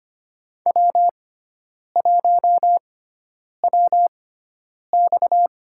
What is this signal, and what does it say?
Morse code, heard as a keyed tone, about 25 wpm, spaced as for 9 wpm, 700 Hz. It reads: W1WX
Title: USA Call Signs - 4 Characters - 6x Character Spacing - 25wpm